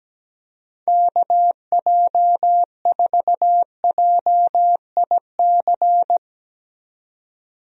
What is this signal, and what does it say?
Morse code, heard as a keyed tone, about 17 wpm, 700 Hz. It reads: KJ4JIC
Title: USA Call Signs - 6 Characters - 17wpm